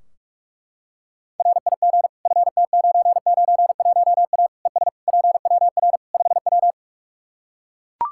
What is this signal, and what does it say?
Morse code, heard as a keyed tone, about 42 wpm, 700 Hz. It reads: RIG FT991A ES PWR 5W
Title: Practice QSO - Exchange 2 - Info - 45wpm